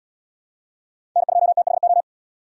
Morse code, heard as a keyed tone, about 67 wpm, 700 Hz. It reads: N2THO